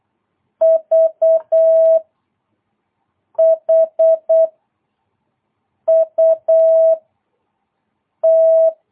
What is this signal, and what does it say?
Morse code, heard as a keyed tone, about 8 wpm, 660 Hz. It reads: V H U T
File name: jh8cbh / 2022cw